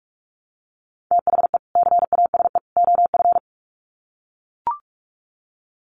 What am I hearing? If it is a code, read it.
Morse code, heard as a keyed tone, about 44 wpm, 700 Hz. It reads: THE CASE OF
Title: Sets of 3 Words 45wpm